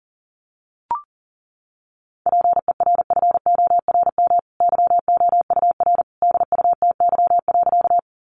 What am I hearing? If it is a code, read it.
Morse code, heard as a keyed tone, about 40 wpm, 700 Hz. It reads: PERFORM YOUR DUTY.
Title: Sentences from Top 1500 Words 40wpm